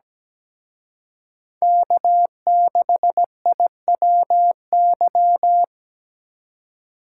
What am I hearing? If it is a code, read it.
Morse code, heard as a keyed tone, about 20 wpm, 700 Hz. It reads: K6IWY
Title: Speed Racing - USA 1x3 Call Signs 17wpm